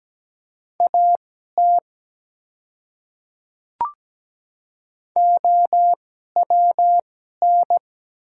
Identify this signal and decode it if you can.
Morse code, heard as a keyed tone, about 17 wpm, spaced as for 12 wpm, 700 Hz. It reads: AT OWN
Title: 2x Character Spacing - Top 300 Words - 17wpm